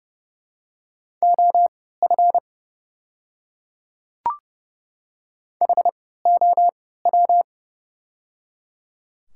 Morse code, heard as a keyed tone, about 30 wpm, spaced as for 17 wpm, 700 Hz. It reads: OF HOW
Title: 3x Character Spacing - Top 100 Words - 30wpm